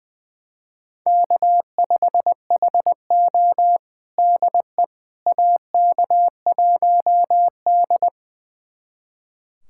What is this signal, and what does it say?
Morse code, heard as a keyed tone, about 20 wpm, 700 Hz. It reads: K5HO DE AK1D
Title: USA Callsign DE Callsign - 4 Characters - 20wpm